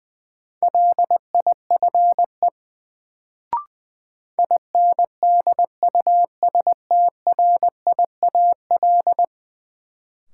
Morse code, heard as a keyed tone, about 20 wpm, 700 Hz. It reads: LIFE INDUSTRIAL
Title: Sets of 5 Words - New Word Drill - 20wpm